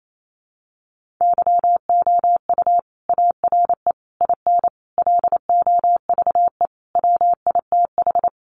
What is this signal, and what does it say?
Morse code, heard as a keyed tone, about 28 wpm, 700 Hz. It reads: YOU ARE IN LOVE WITH
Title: Sentences from Top 400 Words 28wpm